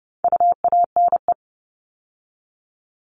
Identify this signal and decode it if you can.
Morse code, heard as a keyed tone, about 30 wpm, 700 Hz. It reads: UANE
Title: All Cut Numbers - 4-Digit 30wpm